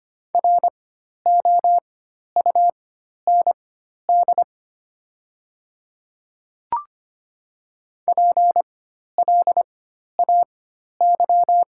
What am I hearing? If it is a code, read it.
Morse code, heard as a keyed tone, about 25 wpm, spaced as for 12 wpm, 690 Hz. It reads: ROUND PLAY
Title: 4x Character Spacing - Top 300 Words - 25wpm